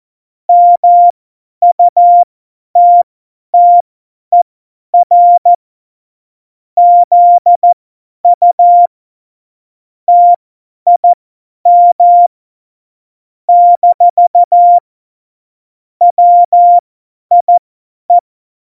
Morse code, heard as a keyed tone, about 14 wpm, spaced as for 10 wpm, 700 Hz. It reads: MUTTER ZU TIM - WIE
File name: Morsekurs / Witze / Witz 14. November 2017, 50 F70